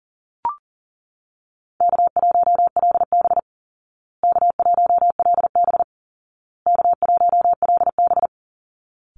Morse code, heard as a keyed tone, about 40 wpm, 700 Hz. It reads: K1LB K1LB K1LB
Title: Easy Call Signs Repeated 3x - 40wpm